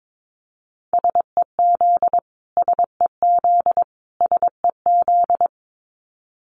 Morse code, heard as a keyed tone, about 22 wpm, 700 Hz. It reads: SEZ SEZ SEZ
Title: Abbreviations Repeated 3x - 22wpm